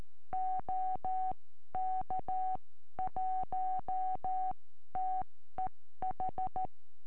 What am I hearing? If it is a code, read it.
Morse code, heard as a keyed tone, about 13 wpm, spaced as for 12 wpm, 750 Hz. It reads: OK1TEH
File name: eme / sm4ive 70cm